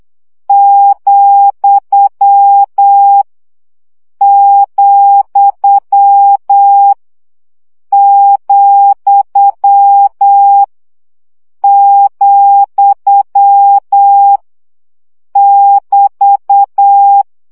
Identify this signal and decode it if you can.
Morse code, heard as a keyed tone, about 8 wpm, 800 Hz. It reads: , , , , <BT>
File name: CW gyakorlas vesszo 7 baud